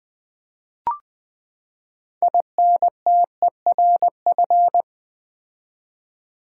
Morse code, heard as a keyed tone, about 20 wpm, 700 Hz. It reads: INTERF
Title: Build A Word - 6 Letters At A Time - 12-Letter Words - 20wpm